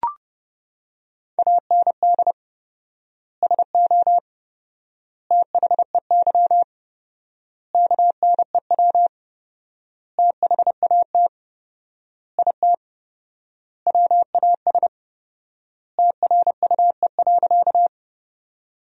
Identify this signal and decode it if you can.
Morse code, heard as a keyed tone, about 30 wpm, 700 Hz. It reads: AND SO THEY KNEW THAT IT WAS TRUE.
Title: 4x Word-Spacing - Sentences from Top 400 Words - 30wpm